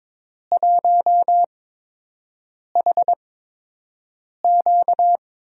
Morse code, heard as a keyed tone, about 22 wpm, spaced as for 6 wpm, 700 Hz. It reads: 1HQ
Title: USA Call Signs - 8x Character Spacing - 22wpm